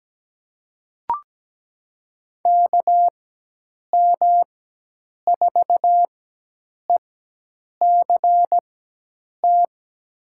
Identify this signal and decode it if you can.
Morse code, heard as a keyed tone, about 17 wpm, spaced as for 8 wpm, 700 Hz. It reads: KM4ECT
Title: USA Call Signs - 6 Characters - 4x Character Spacing - 17wpm